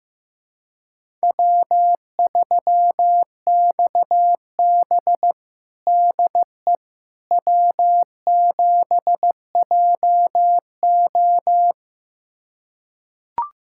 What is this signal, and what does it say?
Morse code, heard as a keyed tone, about 15 wpm, 700 Hz. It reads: W3XB DE W7JO
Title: USA Callsign DE Callsign - 4 Characters - 15wpm